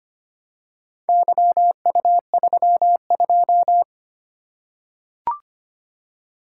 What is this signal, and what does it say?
Morse code, heard as a keyed tone, about 25 wpm, 700 Hz. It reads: YU32
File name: practice / 25wpm / Four Letter-Number Combinations - 25wpm